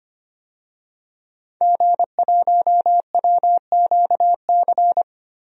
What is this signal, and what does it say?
Morse code, heard as a keyed tone, about 25 wpm, 700 Hz. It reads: G1WQC
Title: DXCC-100 Call Signs - 5 Characters - Repeated 2x - 25wpm